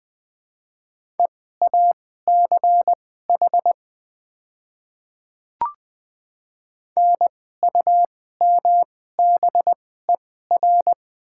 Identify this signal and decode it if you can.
Morse code, heard as a keyed tone, about 20 wpm, spaced as for 14 wpm, 690 Hz. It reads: EACH NUMBER
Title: 2x Character Spacing - Top 200 Words - 20wpm